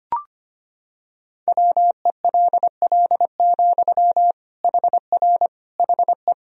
Morse code, heard as a keyed tone, about 25 wpm, 700 Hz. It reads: WELL, HR HE
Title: Sentences from Top 200 Words and Abbreviations 25wpm